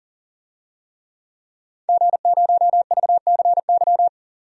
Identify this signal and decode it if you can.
Morse code, heard as a keyed tone, about 40 wpm, 700 Hz. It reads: G0VCY
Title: DXCC-100 Call Signs - 5 Characters - 40wpm